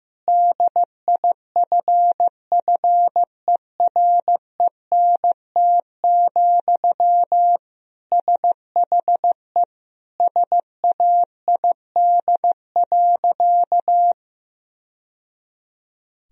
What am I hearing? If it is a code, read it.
Morse code, heard as a keyed tone, about 15 wpm, 700 Hz. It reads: DIFFERENT, SHE SAID.
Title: Sentences from Top 1000 Words - Review Part 3 - 15wpm